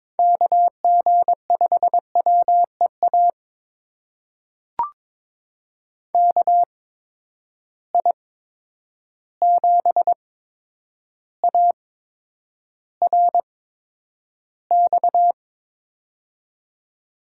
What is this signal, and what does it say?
Morse code, heard as a keyed tone, about 22 wpm, spaced as for 9 wpm, 700 Hz. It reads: KG5WEA KI7ARX
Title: USA Call Signs - 6 Characters - 8x Character Spacing - 22wpm